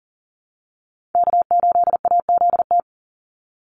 Morse code, heard as a keyed tone, about 40 wpm, 700 Hz. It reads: K8AZT
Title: Call Signs Repeated 2x - 40wpm